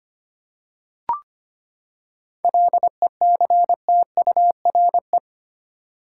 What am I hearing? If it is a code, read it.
Morse code, heard as a keyed tone, about 25 wpm, 700 Hz. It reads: LECTURE
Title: Seven Letter Words - 25wpm